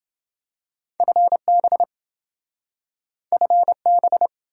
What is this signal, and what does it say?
Morse code, heard as a keyed tone, about 27 wpm, 700 Hz. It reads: FB FB